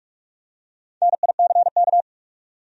Morse code, heard as a keyed tone, about 46 wpm, 700 Hz. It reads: NICK
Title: USA Common Names - Max 5 Letters - 45wpm